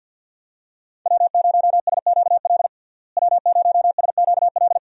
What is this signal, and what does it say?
Morse code, heard as a keyed tone, about 48 wpm, 700 Hz. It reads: W0SQL W0SQL
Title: USA Call Signs - 5 Characters - Repeated 2x - 50wpm